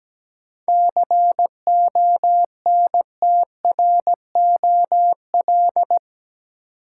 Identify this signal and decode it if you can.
Morse code, heard as a keyed tone, about 17 wpm, 700 Hz. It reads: CONTROL
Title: Sentences from Top 1000 Words - Review Part 2 - New Word Drill - 17wpm